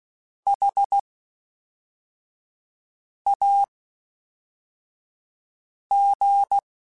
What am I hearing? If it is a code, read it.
Morse code, heard as a keyed tone, about 16 wpm, spaced as for 4 wpm, 800 Hz. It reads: HAG
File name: H Prefixes / H mp3s